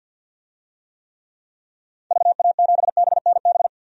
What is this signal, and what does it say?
Morse code, heard as a keyed tone, about 52 wpm, 700 Hz. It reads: VA7BNB
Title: DXCC-100 Call Signs - 6 Characters - 50wpm